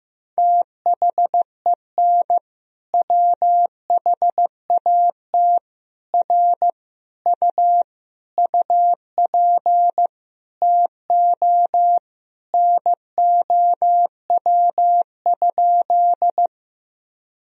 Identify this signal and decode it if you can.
Morse code, heard as a keyed tone, about 15 wpm, 700 Hz. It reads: THEN WHAT R U UP TO NOW?
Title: Sentences from Top 100 Words and Abbreviations 15wpm